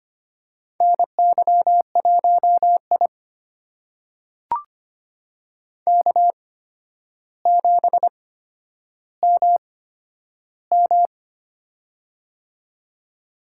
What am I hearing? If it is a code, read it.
Morse code, heard as a keyed tone, about 25 wpm, spaced as for 11 wpm, 700 Hz. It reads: NY1I K7MM
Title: USA Call Signs - 4 Characters - 8x Character Spacing - 25wpm